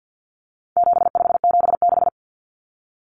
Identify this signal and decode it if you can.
Morse code, heard as a keyed tone, about 50 wpm, 700 Hz. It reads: 7576
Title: Common Cut Numbers - 4-Digit 50wpm